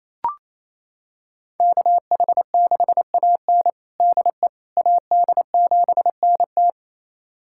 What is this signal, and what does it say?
Morse code, heard as a keyed tone, about 28 wpm, 700 Hz. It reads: KH6AN DE AD7NT
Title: USA Callsign DE Callsign 28wpm